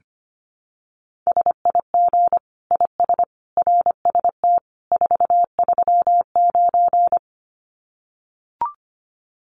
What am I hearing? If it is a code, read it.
Morse code, heard as a keyed tone, about 25 wpm, 700 Hz. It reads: SIG IS RST 439
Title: Common QSO Phrases 25wpm